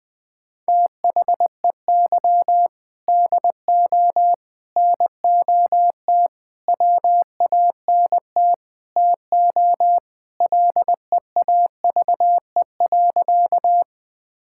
Morse code, heard as a keyed tone, about 21 wpm, 700 Hz. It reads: THEY DO NOT WANT TO LEAVE.
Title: Sentences with 3-Word Phrases - 20wpm